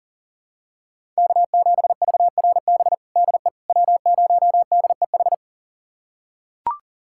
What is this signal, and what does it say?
Morse code, heard as a keyed tone, about 37 wpm, 700 Hz. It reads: K7VRB DE W0DEH
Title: USA Callsign DE Callsign - 5 Characters - 40wpm